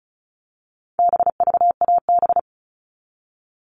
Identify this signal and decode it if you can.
Morse code, heard as a keyed tone, about 34 wpm, 700 Hz. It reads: BVAB
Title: All Cut Numbers - 4-Digit 35wpm